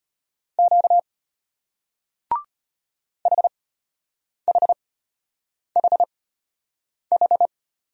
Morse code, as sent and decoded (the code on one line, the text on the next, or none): --.- / .... .... .... ....
Q HHHH